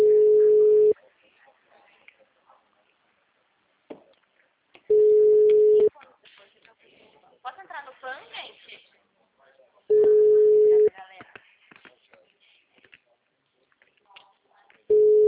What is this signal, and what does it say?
Ringback tone (Europe, 425 Hz), with speech.
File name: out- -2017- - - .25127